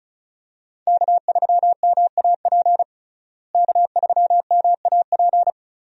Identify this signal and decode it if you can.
Morse code, heard as a keyed tone, about 34 wpm, 700 Hz. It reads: K3MAP K3MAP
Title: USA Call Signs - 5 Characters - Repeated 2x - 35wpm